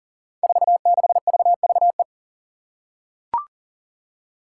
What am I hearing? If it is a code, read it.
Morse code, heard as a keyed tone, about 39 wpm, 700 Hz. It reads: 46VVE
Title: Five Challenging Letter-Number Combinations 40wpm